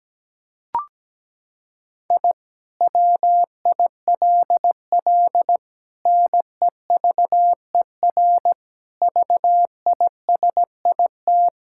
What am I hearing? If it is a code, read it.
Morse code, heard as a keyed tone, about 17 wpm, 700 Hz. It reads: I WILL NEVER VISIT
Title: Sentences from Top 1000 Words - Review Part 6 - 17wpm